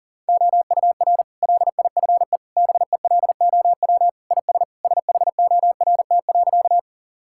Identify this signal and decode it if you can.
Morse code, heard as a keyed tone, about 37 wpm, 700 Hz. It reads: OUR LIFE BELOW IS SHORT.
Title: Sentences from Top 1000 Words - Review Part 5 - 40wpm